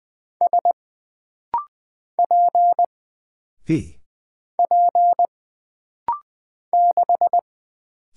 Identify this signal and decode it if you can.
Morse code, heard as a keyed tone, about 20 wpm, 700 Hz. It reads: S P P 6
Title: Single Character (Letter:Number:Punctuation:Prosign) - Rapid-Fire - 20wpm